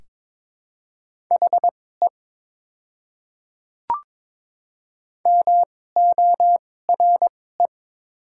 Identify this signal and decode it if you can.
Morse code, heard as a keyed tone, about 22 wpm, spaced as for 16 wpm, 700 Hz. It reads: HE MORE
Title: 2x Character Spacing - Top 100 Words - 22wpm